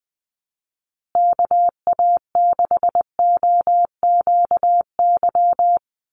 Morse code, heard as a keyed tone, about 20 wpm, 700 Hz. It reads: KA6OQY
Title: USA Call Signs Repeated 2x 20wpm